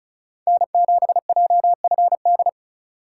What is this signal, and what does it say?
Morse code, heard as a keyed tone, about 35 wpm, 700 Hz. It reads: N7JFD
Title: USA Call Signs - 5 Characters - 2x Character Spacing - 35wpm